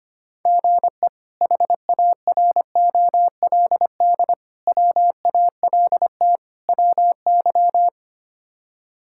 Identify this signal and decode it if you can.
Morse code, heard as a keyed tone, about 25 wpm, 700 Hz. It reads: GE HAROLD WALT WY
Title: K1USN SST QSO Practice - 25wpm